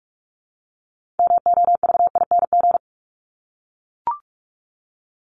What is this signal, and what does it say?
Morse code, heard as a keyed tone, about 44 wpm, 700 Hz. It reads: MOVING
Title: ING Suffix - Top 100 Words with Base Words 45wpm